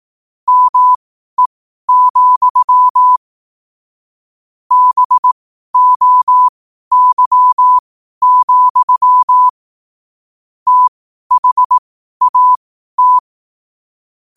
Morse code, heard as a keyed tone, about 18 wpm, spaced as for 12 wpm, 1000 Hz. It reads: ME, BOY, THAT
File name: qotdfiles / 12wpm